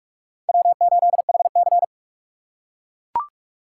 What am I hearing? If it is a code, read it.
Morse code, heard as a keyed tone, about 44 wpm, 700 Hz. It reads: W8HC
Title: USA Call Signs - 4 Characters - 2x Character Spacing - 45wpm